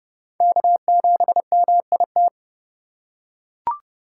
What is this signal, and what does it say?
Morse code, heard as a keyed tone, about 31 wpm, 700 Hz. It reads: K7MIT
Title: USA Call Signs - 5 Characters - 3x Character Spacing - 30wpm